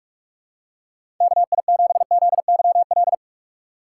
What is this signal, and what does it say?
Morse code, heard as a keyed tone, about 46 wpm, 700 Hz. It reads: KI7ZYL
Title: USA Call Signs - 6 Characters - 6x Character Spacing - 45wpm